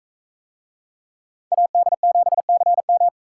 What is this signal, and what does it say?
Morse code, heard as a keyed tone, about 44 wpm, 700 Hz. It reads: AD7CM